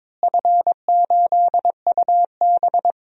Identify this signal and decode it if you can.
Morse code, heard as a keyed tone, about 22 wpm, 700 Hz. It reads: F8UB
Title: DXCC-100 Call Signs - 4 Characters - Repeated 2x - 22wpm